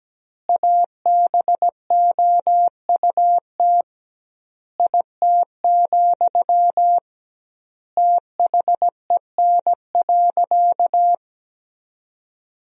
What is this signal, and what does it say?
Morse code, heard as a keyed tone, about 17 wpm, 700 Hz. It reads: ABOUT IT, THEN.
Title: 2x Word-Spacing - Sentences from Top 200 Words - 17wpm